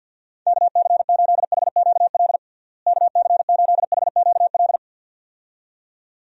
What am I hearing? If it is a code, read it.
Morse code, heard as a keyed tone, about 51 wpm, 700 Hz. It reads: KC8HQL KC8HQL
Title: USA Call Signs - 6 Characters - Repeated 2x - 50wpm